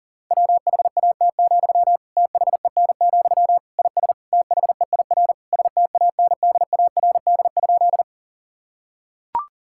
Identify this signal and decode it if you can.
Morse code, heard as a keyed tone, about 40 wpm, 700 Hz. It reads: WHAT, THEN, IS THEIR STANDARD?
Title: Sentences from Top 1000 Words - Review Part 6 - 40wpm